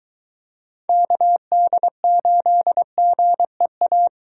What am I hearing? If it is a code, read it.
Morse code, heard as a keyed tone, about 23 wpm, 700 Hz. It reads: KD8GEA